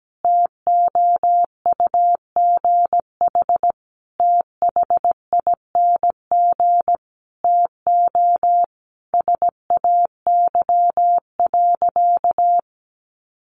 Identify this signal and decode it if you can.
Morse code, heard as a keyed tone, about 17 wpm, 700 Hz. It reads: TOUGH THING TO SAY.
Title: Sentences from Top 1200 Words 17wpm